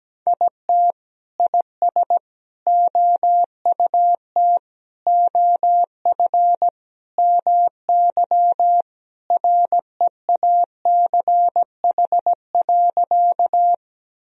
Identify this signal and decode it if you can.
Morse code, heard as a keyed tone, about 17 wpm, 700 Hz. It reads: IT IS OUT OF MY REACH.
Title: Sentences from Top 1000 Words - Review Part 2 - 17wpm